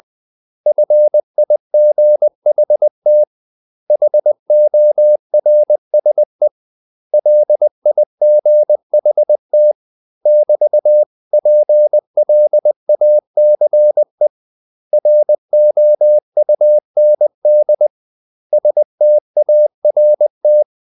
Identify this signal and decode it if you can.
Morse code, heard as a keyed tone, about 20 wpm, 600 Hz. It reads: FIGHT HORSE LIGHT <BT>PLACE ROUND START
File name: CW / Practice / 500MostCommonEnglishWords20